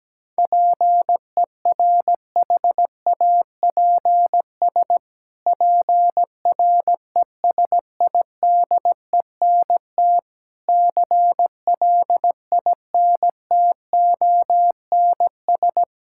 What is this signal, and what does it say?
Morse code, heard as a keyed tone, about 17 wpm, 700 Hz. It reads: PERHAPS PRESIDENT CLINTONS
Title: Pangram Challenge v2 - 17wpm